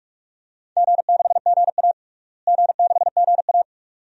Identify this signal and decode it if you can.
Morse code, heard as a keyed tone, about 46 wpm, 700 Hz. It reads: G6GA G6GA
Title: DXCC-100 Call Signs - 4 Characters - Repeated 2x - 45wpm